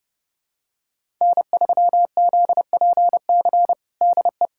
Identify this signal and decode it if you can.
Morse code, heard as a keyed tone, about 30 wpm, 700 Hz. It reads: N3ZPC DE
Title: USA Callsign DE Callsign - 5 Characters - 30wpm